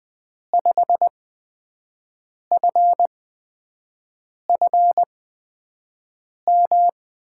Morse code, heard as a keyed tone, about 20 wpm, spaced as for 5 wpm, 700 Hz. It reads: 5FFM
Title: USA Call Signs - 6 Characters - 8x Character Spacing - 20wpm